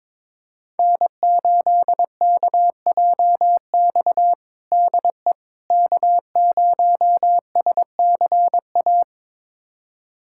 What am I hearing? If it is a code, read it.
Morse code, heard as a keyed tone, about 22 wpm, 700 Hz. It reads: N8KJX DE K0SCA
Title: USA Callsign DE Callsign - 5 Characters - 22wpm